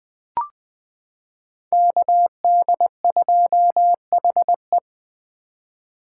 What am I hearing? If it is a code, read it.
Morse code, heard as a keyed tone, about 20 wpm, 700 Hz. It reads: KD2HE
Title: K1USN SST QSO Practice - 20wpm